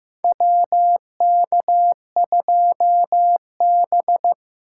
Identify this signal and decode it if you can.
Morse code, heard as a keyed tone, about 15 wpm, 700 Hz. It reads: WK2B
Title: USA Call Signs - 4 Characters - 3x Character Spacing - 15wpm